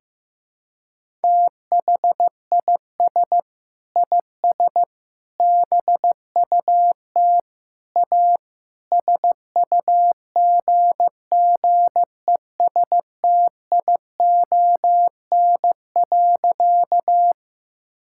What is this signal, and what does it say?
Morse code, heard as a keyed tone, about 15 wpm, 700 Hz. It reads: THIS IS BUT A SUGGESTION.
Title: Sentences from Top 1600 Words 15wpm